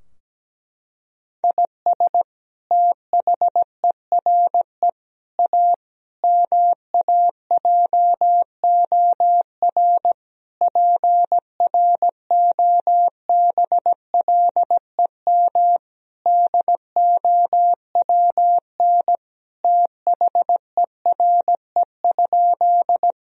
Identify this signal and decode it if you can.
Morse code, heard as a keyed tone, about 17 wpm, 700 Hz. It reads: IS THERE A MAJOR PROBLEM DOWN THERE?
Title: Sentences from Top 1000 Words - Review Part 4 - 17wpm